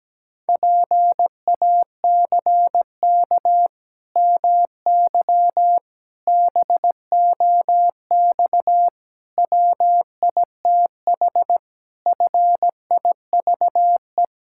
Morse code, heard as a keyed tone, about 17 wpm, 700 Hz. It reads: PACK MY BOX WITH FIVE
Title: Pangram Challenge v2 - 17wpm